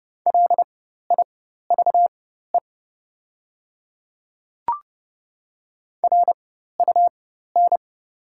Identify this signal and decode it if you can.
Morse code, heard as a keyed tone, about 30 wpm, spaced as for 14 wpm, 700 Hz. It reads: LIVE RUN
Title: 4x Character Spacing - Top 300 Words - 30wpm